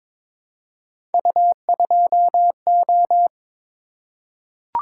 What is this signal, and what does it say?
Morse code, heard as a keyed tone, about 22 wpm, 700 Hz. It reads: U2O
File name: practice / 22wpm / Three Letter-Number Combinations - 22wpm